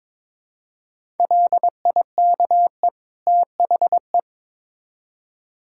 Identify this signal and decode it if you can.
Morse code, heard as a keyed tone, about 21 wpm, 700 Hz. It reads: LIKE THE
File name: Build A Phrase - 2-Word Phrases - 22wpm